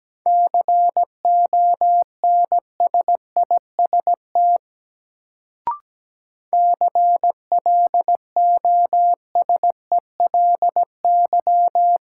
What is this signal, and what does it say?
Morse code, heard as a keyed tone, about 17 wpm, 700 Hz. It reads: CONSIST CLOSELY
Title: Sentences from Top 1400 Words - New Word Drill - 17wpm